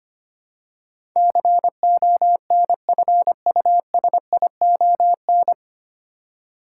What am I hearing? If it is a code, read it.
Morse code, heard as a keyed tone, about 25 wpm, 700 Hz. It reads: CONFUSION
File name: practice / 25wpm / 9 Letter Words - First 3 Letters Given - 25wpm